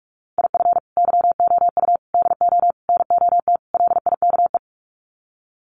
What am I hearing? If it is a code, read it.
Morse code, heard as a keyed tone, about 44 wpm, 700 Hz. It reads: IF YOU DO NOT LIKE
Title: Sets of 5 Words 45wpm